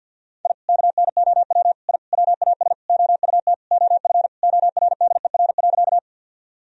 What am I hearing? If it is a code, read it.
Morse code, heard as a keyed tone, about 52 wpm, 700 Hz. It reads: I KNOW I WAS OUT OF ORDER.
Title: Sentences from Top 1000 Words - Review Part 2 - 50wpm